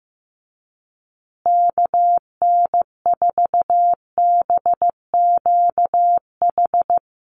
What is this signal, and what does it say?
Morse code, heard as a keyed tone, about 14 wpm, 700 Hz. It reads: KN4BQH
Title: USA Call Signs Repeated 2x 15wpm